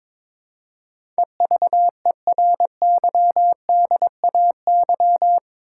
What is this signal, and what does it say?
Morse code, heard as a keyed tone, about 23 wpm, 700 Hz. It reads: EVERYDAY
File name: practice / 22wpm / Eight Letter Words - 22wpm